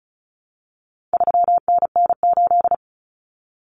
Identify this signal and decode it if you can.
Morse code, heard as a keyed tone, about 34 wpm, 700 Hz. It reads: 3NN8
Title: Common Cut Numbers - 4-Digit 35wpm